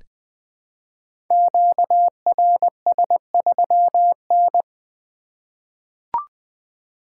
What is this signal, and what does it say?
Morse code, heard as a keyed tone, about 20 wpm, 700 Hz. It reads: QRS3N
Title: Code Groups - 5 Letter:Numbers - 20wpm